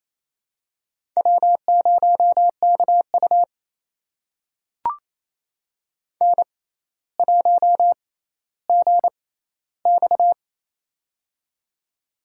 Morse code, heard as a keyed tone, about 27 wpm, spaced as for 14 wpm, 700 Hz. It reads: W0KU N1GX
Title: USA Call Signs - 6x Character Spacing - 28wpm